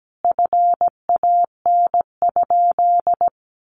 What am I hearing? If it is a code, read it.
Morse code, heard as a keyed tone, about 17 wpm, 700 Hz. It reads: FAN?
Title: Sentences from Top 1300 Words 17wpm